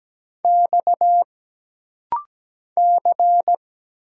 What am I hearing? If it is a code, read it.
Morse code, heard as a keyed tone, about 17 wpm, 700 Hz. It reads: X C
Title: Single Letters - Rapid-Fire 17wpm